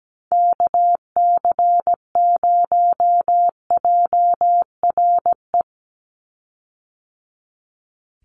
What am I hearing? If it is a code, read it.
Morse code, heard as a keyed tone, about 17 wpm, 700 Hz. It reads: KC0JRE
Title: USA Call Signs 17wpm